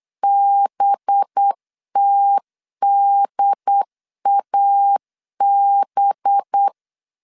Morse code, heard as a keyed tone, about 8 wpm, 800 Hz. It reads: BTDAB
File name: LEZIONI / 8 YJ45 / 8 Y 8wpm 3245